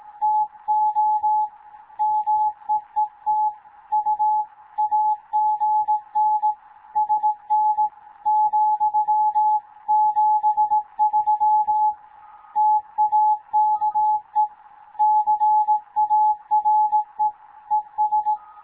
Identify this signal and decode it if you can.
Morse code, heard as a keyed tone, about 18 wpm, 850 Hz. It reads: TO MEET U AGN SN ,73 TAKE CARE ES
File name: Morze-HA2MN / hangp / 2 11 GYAKORLAT